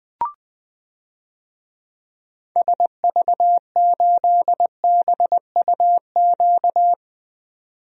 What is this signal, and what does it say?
Morse code, heard as a keyed tone, about 21 wpm, 700 Hz. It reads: SV8BUQ
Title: DXCC-100 Call Signs - 6 Characters - 20wpm